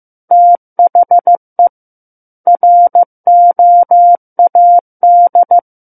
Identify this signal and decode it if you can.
Morse code, heard as a keyed tone, about 15 wpm, 700 Hz. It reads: THE ROAD
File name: Book One / 06-Chapter 6 - 15 WPM